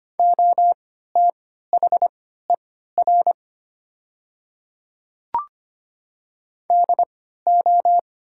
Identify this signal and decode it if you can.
Morse code, heard as a keyed tone, about 25 wpm, spaced as for 14 wpm, 700 Hz. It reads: OTHER DO
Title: 3x Character Spacing - Top 300 Words - 25wpm